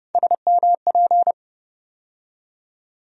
Morse code, heard as a keyed tone, about 31 wpm, 700 Hz. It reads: SMP
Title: USA Call Sign Postfixes - 3 Letters 30wpm